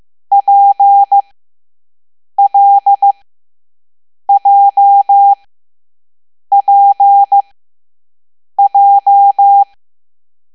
Morse code, heard as a keyed tone, about 15 wpm, spaced as for 6 wpm, 800 Hz. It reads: PLJPJ
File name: Morse / Level7B